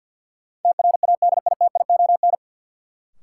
Morse code, heard as a keyed tone, about 53 wpm, 700 Hz. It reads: TRADITION
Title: Sentences from Top 1500 Words - New Word Drill Alt - 50wpm